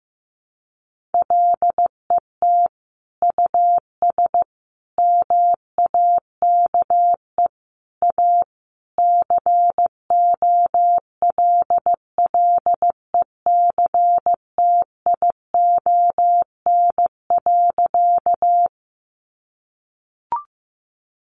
Morse code, heard as a keyed tone, about 15 wpm, 700 Hz. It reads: LET US MAKE A COLLECTION.
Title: Sentences from Top 1200 Words 15wpm